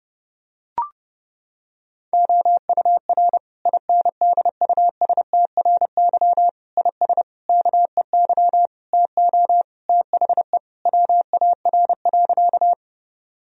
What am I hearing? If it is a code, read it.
Morse code, heard as a keyed tone, about 30 wpm, 700 Hz. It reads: OUR INDUSTRY IS KEY TO THE WAR.
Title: Sentences from Top 1000 Words - Review Part 4 - 30wpm